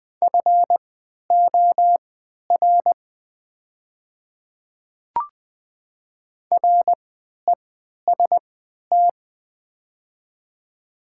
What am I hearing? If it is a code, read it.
Morse code, heard as a keyed tone, about 20 wpm, spaced as for 11 wpm, 700 Hz. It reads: FOR REST
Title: 3x Character Spacing - Top 500 Words - 20wpm